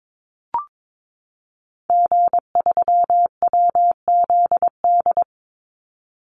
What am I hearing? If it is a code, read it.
Morse code, heard as a keyed tone, about 22 wpm, 700 Hz. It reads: G3WZD
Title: Call Signs Repeated 2x - 22wpm